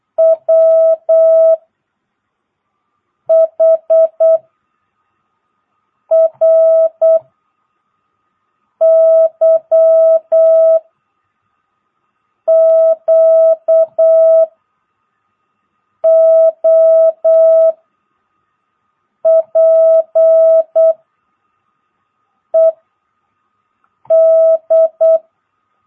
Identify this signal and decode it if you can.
Morse code, heard as a keyed tone, about 8 wpm, 650 Hz. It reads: W H R Y Q O P E D
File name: jh8cbh / 2022cw